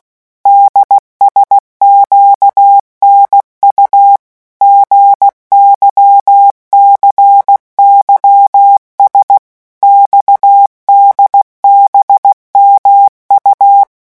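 Morse code, heard as a keyed tone, about 16 wpm, 800 Hz. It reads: DSQNU GYCYS XDBMU
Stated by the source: Morse letter-group revision drill at 15 words per minute.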